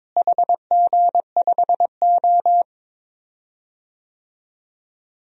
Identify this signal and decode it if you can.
Morse code, heard as a keyed tone, about 22 wpm, 700 Hz. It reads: HG5O
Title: DXCC-100 Call Signs - 4 Characters - Repeated 2x - 22wpm